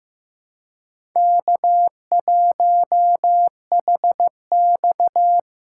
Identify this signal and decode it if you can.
Morse code, heard as a keyed tone, about 15 wpm, 700 Hz. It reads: K1HX
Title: USA Call Signs - 4 Characters - 3x Character Spacing - 15wpm